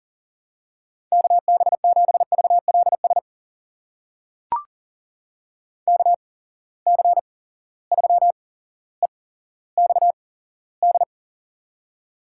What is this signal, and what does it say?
Morse code, heard as a keyed tone, about 40 wpm, spaced as for 17 wpm, 700 Hz. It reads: KB7VLS KC3EXD
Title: USA Call Signs - 6 Characters - 8x Character Spacing - 40wpm